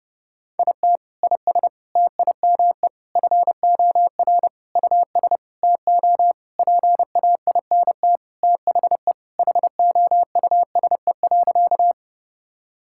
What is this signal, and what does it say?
Morse code, heard as a keyed tone, about 30 wpm, 700 Hz. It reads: IT IS TIME FOR US TO PAINT THE HOUSE.
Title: Sentences from Top 1000 Words - Review Part 6 - 30wpm